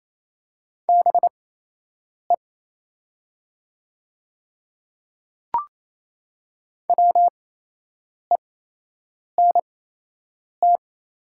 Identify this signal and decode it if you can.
Morse code, heard as a keyed tone, about 28 wpm, spaced as for 8 wpm, 700 Hz. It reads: BE WENT
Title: 8x Character Spacing - Top 300 Words - 28wpm